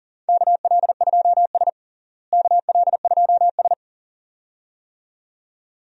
Morse code, heard as a keyed tone, about 37 wpm, 700 Hz. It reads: KL2S KL2S
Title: USA Call Signs - 4 Characters - Repeated 2x - 40wpm